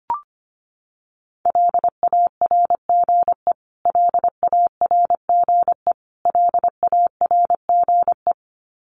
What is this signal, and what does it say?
Morse code, heard as a keyed tone, about 25 wpm, 700 Hz. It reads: LARGE LARGE LARGE
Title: Top 500 Words Repeated 25wpm